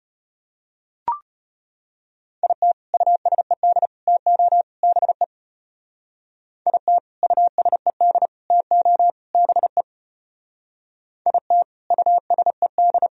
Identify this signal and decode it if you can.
Morse code, heard as a keyed tone, about 31 wpm, 700 Hz. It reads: IT USED TO BE IT USED TO BE IT USED
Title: Speed-Racing - Sets of 4 Words 25wpm